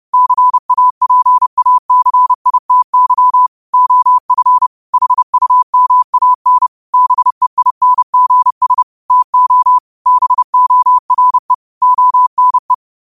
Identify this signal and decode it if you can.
Morse code, heard as a keyed tone, about 30 wpm, 1000 Hz. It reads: CAPACITY OF HUMAN BEINGS TO BORE ONE
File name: qotdfiles / 30wpm